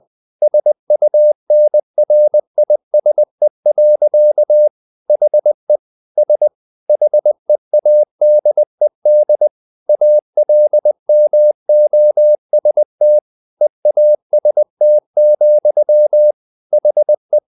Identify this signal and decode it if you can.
Morse code, heard as a keyed tone, about 20 wpm, 600 Hz. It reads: SUNRISE. HE S HEADED ALMOST EAST, HE
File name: the old man and the sea part 2 20wpm